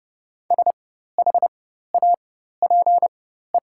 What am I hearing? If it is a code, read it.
Morse code, heard as a keyed tone, about 30 wpm, spaced as for 14 wpm, 700 Hz. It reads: SHAPE
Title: 4x Character Spacing - Top 500 Words - 30wpm